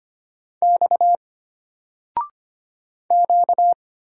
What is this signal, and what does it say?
Morse code, heard as a keyed tone, about 25 wpm, 700 Hz. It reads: X Q
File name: Single-Letter-Number-25wpm